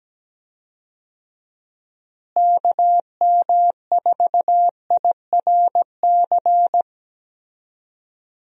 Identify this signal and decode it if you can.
Morse code, heard as a keyed tone, about 17 wpm, 700 Hz. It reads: KM4IRC
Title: USA Call Signs - 6 Characters - Repeated 3x - 17wpm